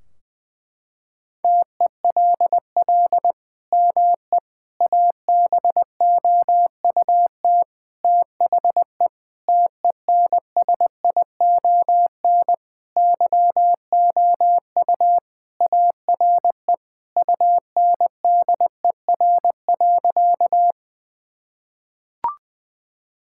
Morse code, heard as a keyed tone, about 20 wpm, 700 Hz. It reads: TELL ME ABOUT THE TENSION YOU ARE UNDER.
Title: Sentences from Top 1600 Words 20wpm